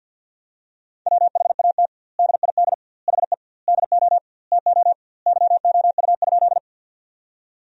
Morse code, heard as a keyed tone, about 50 wpm, 700 Hz. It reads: WHAT DID HE DO TO YOU?
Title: 2x Word-Spacing - Sentences from Top 100 Words - 50wpm